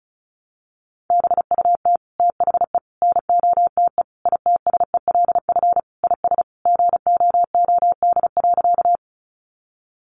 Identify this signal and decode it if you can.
Morse code, heard as a keyed tone, about 34 wpm, 700 Hz. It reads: BUT THE NOTE ITSELF IS GOOD.
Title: Sentences from Top 800 Words 35wpm